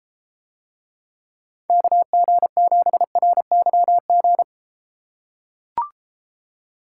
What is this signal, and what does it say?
Morse code, heard as a keyed tone, about 32 wpm, 700 Hz. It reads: KG7RYG